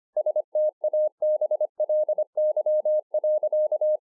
Morse code, heard as a keyed tone, about 25 wpm, 610 Hz. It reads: STABLY.